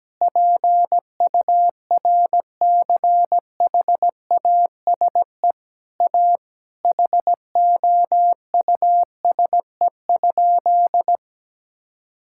Morse code, heard as a keyed tone, about 17 wpm, 700 Hz. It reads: PURCHASE A HOUSE?
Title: Sentences from Top 1000 Words - Review Part 6 - 17wpm